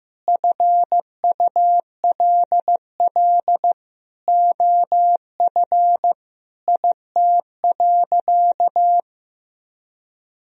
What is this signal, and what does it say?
Morse code, heard as a keyed tone, about 15 wpm, 700 Hz. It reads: FULL OF IT.
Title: Sentences from Top 1000 Words - Review Part 3 - 15wpm